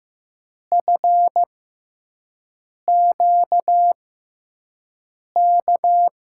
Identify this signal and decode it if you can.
Morse code, heard as a keyed tone, about 15 wpm, 700 Hz. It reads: F Q K
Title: USA Call Signs - 6x Character Spacing - 15wpm